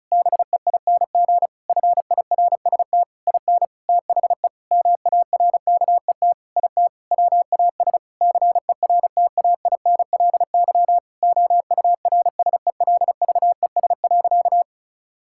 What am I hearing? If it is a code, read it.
Morse code, heard as a keyed tone, about 34 wpm, 700 Hz. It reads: BEING FIRST IN THE MARKET IT WAS CERTAINLY OURSELVES.
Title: Sentences from Top 1600 Words 35wpm